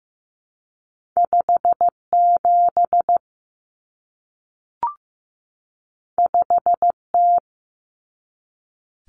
Morse code, heard as a keyed tone, about 15 wpm, 700 Hz. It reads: 57 5T
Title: Common Cut Numbers - 2-Digit 15wpm